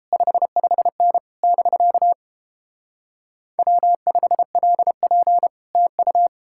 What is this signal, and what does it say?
Morse code, heard as a keyed tone, about 29 wpm, 700 Hz. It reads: 55N <BK> W5LP TU